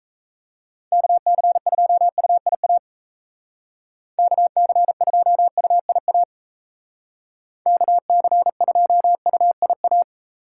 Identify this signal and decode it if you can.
Morse code, heard as a keyed tone, about 36 wpm, 700 Hz. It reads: KC2UIA KC2UIA KC2UIA